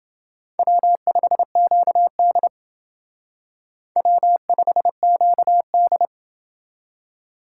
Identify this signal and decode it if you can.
Morse code, heard as a keyed tone, about 28 wpm, 700 Hz. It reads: W5QD W5QD